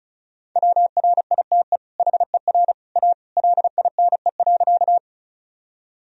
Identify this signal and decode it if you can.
Morse code, heard as a keyed tone, about 34 wpm, 700 Hz. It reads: WRITE HER A LINE.
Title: Sentences from Top 1000 Words - Review Part 1 - 35wpm